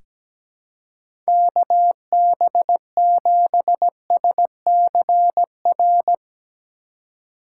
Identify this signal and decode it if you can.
Morse code, heard as a keyed tone, about 17 wpm, 700 Hz. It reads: KB7SCR
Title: USA Call Signs - 3x Character Spacing - 17wpm